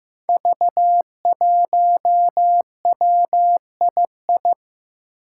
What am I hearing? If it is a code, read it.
Morse code, heard as a keyed tone, about 15 wpm, 700 Hz. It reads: V1WII
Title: Five Challenging Letter-Number Combinations 15wpm